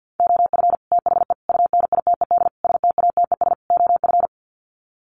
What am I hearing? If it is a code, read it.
Morse code, heard as a keyed tone, about 50 wpm, 700 Hz. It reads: OF THE UNITED STATES OF
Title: Sets of 5 Words Encore 50wpm